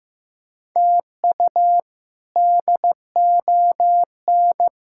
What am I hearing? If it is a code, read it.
Morse code, heard as a keyed tone, about 15 wpm, 700 Hz. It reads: TU DON
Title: K1USN SST QSO Practice - 15wpm